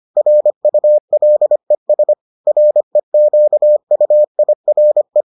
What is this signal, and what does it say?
Morse code, heard as a keyed tone, about 25 wpm, 600 Hz. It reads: RULES REQUIRE